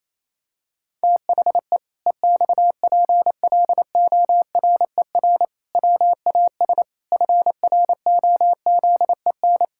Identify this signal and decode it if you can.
Morse code, heard as a keyed tone, about 27 wpm, 700 Hz. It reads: THE EXPLORER WAS FROZEN
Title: Pangram Challenge v2 - 28wpm